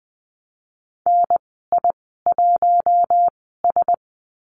Morse code, heard as a keyed tone, about 20 wpm, spaced as for 14 wpm, 700 Hz. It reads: NI1S
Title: 048 Lesson 9g - Number 4 - Call Signs 20wpm